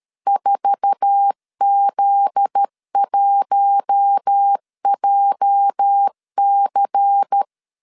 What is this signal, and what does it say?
Morse code, heard as a keyed tone, about 13 wpm, 800 Hz. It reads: 4Z1JC
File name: LEZIONI / 8 YJ45 / 8 YJ4 12wpm 2151